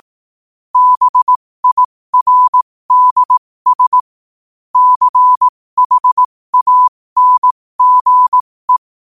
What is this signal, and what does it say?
Morse code, heard as a keyed tone, about 18 wpm, spaced as for 15 wpm, 1000 Hz. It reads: BIRDS CHANGE
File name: qotdfiles / 15wpm